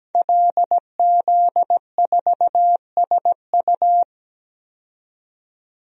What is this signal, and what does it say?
Morse code, heard as a keyed tone, about 17 wpm, 700 Hz. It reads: LZ4SU
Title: DXCC-100 Call Signs - 5 Characters - 17wpm